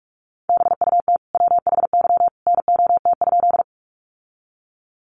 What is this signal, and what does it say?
Morse code, heard as a keyed tone, about 44 wpm, 700 Hz. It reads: BUT WHY NOT?
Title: Sentences from Top 600 Words 45wpm